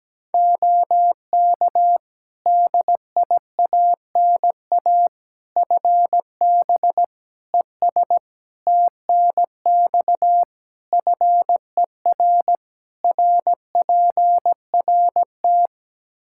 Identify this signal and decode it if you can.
Morse code, heard as a keyed tone, about 17 wpm, 700 Hz. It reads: OK DIANA FB ES TNX FER RPRT
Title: Practice QSO - Exchange 2 - Info - 17wpm